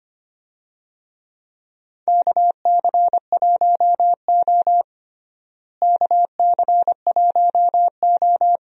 Morse code, heard as a keyed tone, about 25 wpm, 700 Hz. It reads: KC1O KC1O
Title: USA Call Signs - 4 Characters - Repeated 2x - 25wpm